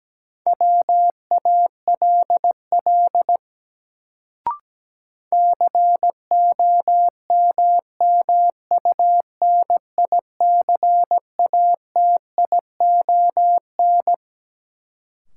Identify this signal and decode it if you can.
Morse code, heard as a keyed tone, about 17 wpm, 700 Hz. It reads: WALL COMMUNICATION
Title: Sentences from Top 1000 Words - Review Part 6 - New Word Drill - 17wpm